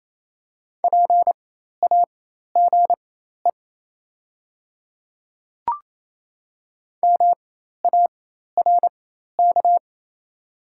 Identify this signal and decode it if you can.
Morse code, heard as a keyed tone, about 28 wpm, spaced as for 13 wpm, 700 Hz. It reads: PAGE MARK
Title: 4x Character Spacing - Top 300 Words - 28wpm